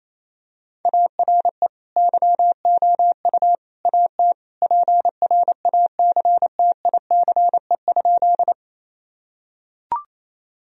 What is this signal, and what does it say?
Morse code, heard as a keyed tone, about 28 wpm, 700 Hz. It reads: ARE YOU AT PRACTICE?
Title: Sentences from Top 1000 Words - Review Part 4 - 28wpm